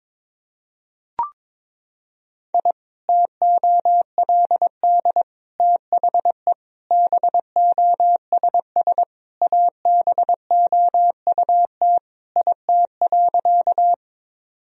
Morse code, heard as a keyed tone, about 22 wpm, 700 Hz. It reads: I TOLD THE BOSS ABOUT IT.
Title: Sentences from Top 1000 Words - Review Part 4 - 22wpm